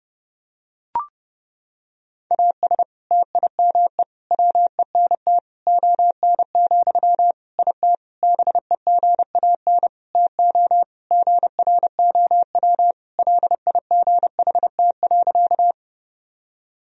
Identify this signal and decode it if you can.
Morse code, heard as a keyed tone, about 32 wpm, 700 Hz. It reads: AS TIME WENT ON, IT BEGAN TO GROW LIGHT.
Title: Sentences from Top 1000 Words - Review Part 1 - 30wpm